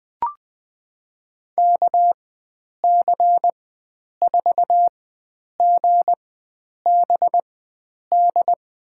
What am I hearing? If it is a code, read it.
Morse code, heard as a keyed tone, about 20 wpm, spaced as for 9 wpm, 700 Hz. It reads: KC4GBD
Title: USA Call Signs - 6 Characters - 4x Character Spacing - 20wpm